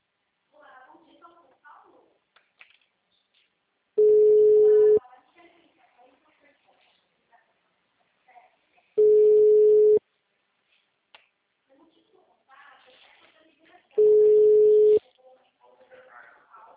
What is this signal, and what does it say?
Ringback tone (Europe, 425 Hz), with speech.